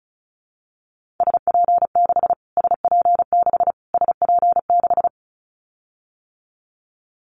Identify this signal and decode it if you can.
Morse code, heard as a keyed tone, about 34 wpm, 700 Hz. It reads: SP6 SP6 SP6
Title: Callsign Prefixes for Top 25 DXCC - Repeated 3x - 35wpm